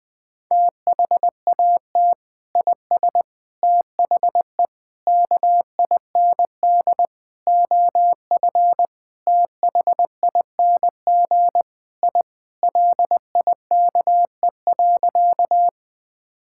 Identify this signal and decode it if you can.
Morse code, heard as a keyed tone, about 20 wpm, 700 Hz. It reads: THAT IS THE KIND OF THING I LIKE.
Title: Sentences with 3-Word Phrases - 20wpm